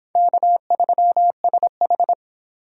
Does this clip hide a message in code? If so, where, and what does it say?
Yes: Morse code from 0.1 to 2.1 s, heard as a keyed tone, about 26 wpm, 700 Hz. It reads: K3SH